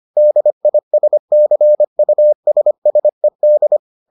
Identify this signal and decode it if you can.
Morse code, heard as a keyed tone, about 25 wpm, 600 Hz. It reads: DISCUSSED